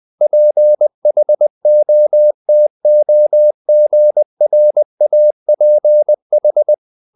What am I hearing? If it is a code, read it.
Morse code, heard as a keyed tone, about 20 wpm, 600 Hz. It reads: PHOTOGRAPH
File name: the old man and the sea / the old man and the sea part 1 20wpm